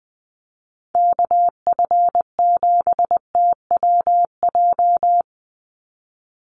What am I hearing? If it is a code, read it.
Morse code, heard as a keyed tone, about 20 wpm, 700 Hz. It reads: KF7TWJ
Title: USA Call Signs 20wpm